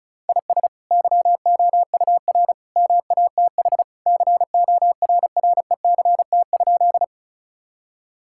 Morse code, heard as a keyed tone, about 36 wpm, 700 Hz. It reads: IS YOUR MATH CORRECT?
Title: Sentences from Top 1000 Words - Review Part 6 - 35wpm